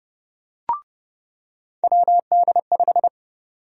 Morse code, heard as a keyed tone, about 30 wpm, 700 Hz. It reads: WD5
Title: USA Call Sign Prefixes - 30wpm